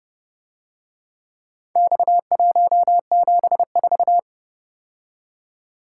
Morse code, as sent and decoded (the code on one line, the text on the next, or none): -..- .---- --... ....-
X174